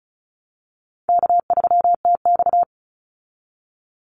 Morse code, heard as a keyed tone, about 34 wpm, 700 Hz. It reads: K3TX
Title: Easy Call Signs Repeated 2x - 35wpm